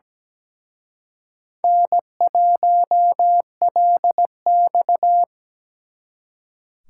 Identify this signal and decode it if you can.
Morse code, heard as a keyed tone, about 20 wpm, 700 Hz. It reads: N1LX
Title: Speed Racing - USA 1x2 Call Signs 17wpm